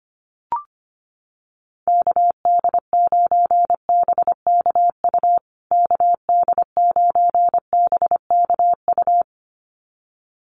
Morse code, heard as a keyed tone, about 25 wpm, 700 Hz. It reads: KD9BKU KD9BKU
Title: USA Call Signs Repeated 2x 25wpm